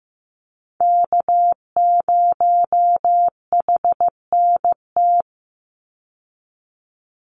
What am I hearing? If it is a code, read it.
Morse code, heard as a keyed tone, about 15 wpm, 700 Hz. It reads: K0HNT
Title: USA Call Signs 15wpm